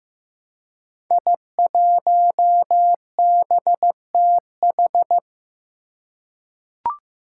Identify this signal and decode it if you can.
Morse code, heard as a keyed tone, about 15 wpm, 700 Hz. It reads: I1BTH
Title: Five Challenging Letter-Number Combinations 15wpm